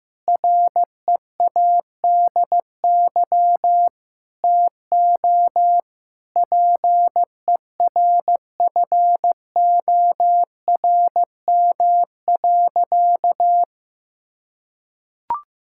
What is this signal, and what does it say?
Morse code, heard as a keyed tone, about 16 wpm, 700 Hz. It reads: READY TO PERFORM.
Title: Sentences from Top 1000 Words - Review Part 5 - 15wpm